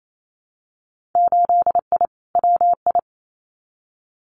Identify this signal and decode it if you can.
Morse code, heard as a keyed tone, about 28 wpm, 700 Hz. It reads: 8I WI
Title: Winter Field Day Exchanges 28wpm